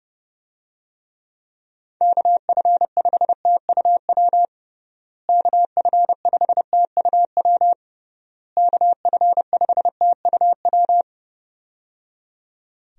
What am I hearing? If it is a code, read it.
Morse code, heard as a keyed tone, about 30 wpm, 700 Hz. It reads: KF5TUW KF5TUW KF5TUW
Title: USA Call Signs - 6 Characters - Repeated 3x - 30wpm